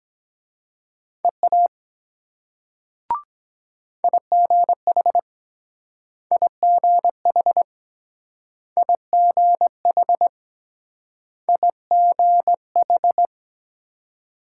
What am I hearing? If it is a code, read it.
Morse code, heard as a keyed tone, about 23 wpm, 700 Hz. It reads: EA IGH IGH IGH IGH